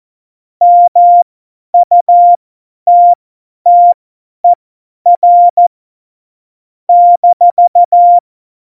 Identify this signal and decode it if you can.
Morse code, heard as a keyed tone, about 14 wpm, spaced as for 10 wpm, 700 Hz. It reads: MUTTER -
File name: Morsekurs / Witze / Witz 26. September 17, Tempo 50